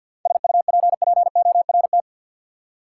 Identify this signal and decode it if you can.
Morse code, heard as a keyed tone, about 47 wpm, 700 Hz. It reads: SUPPORT
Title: Seven Letter Words - 50wpm